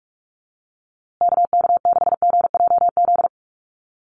Morse code, heard as a keyed tone, about 44 wpm, 690 Hz. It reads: KK6GJZ
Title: USA Call Signs Repeated 3x 45wpm